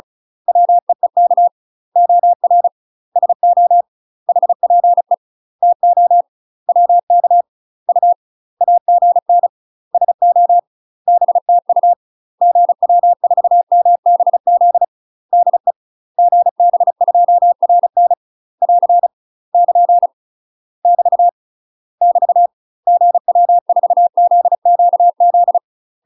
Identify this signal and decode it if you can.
Morse code, heard as a keyed tone, about 34 wpm, 700 Hz. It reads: WEEK OR SO HPE TO WK U AGN SO BTU GW4MBZ DE GB2RN <AR> <KN> <BT> <BT> GW4ZQZ
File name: UK MASTER QSOs TRAINING / 35WPM / UK MASTER QSOs 0015